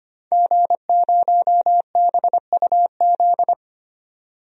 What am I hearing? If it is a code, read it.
Morse code, heard as a keyed tone, about 25 wpm, 700 Hz. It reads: G0BUZ
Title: DXCC-100 Call Signs - 5 Characters - 25wpm